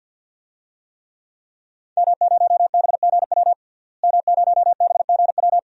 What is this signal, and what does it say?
Morse code, heard as a keyed tone, about 49 wpm, 700 Hz. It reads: M0BGW M0BGW
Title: DXCC-100 Call Signs - 5 Characters - Repeated 2x - 50wpm